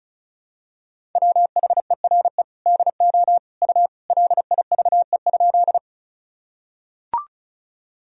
Morse code, heard as a keyed tone, about 36 wpm, 700 Hz. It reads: WHERE DO U LIVE?
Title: Sentences from Top 200 Words and Abbreviations 35wpm